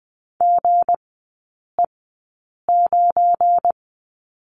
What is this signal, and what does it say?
Morse code, heard as a keyed tone, about 20 wpm, spaced as for 8 wpm, 690 Hz. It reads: GE9
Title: 191 Lesson 31e - Letter v - Three Characters 20wpm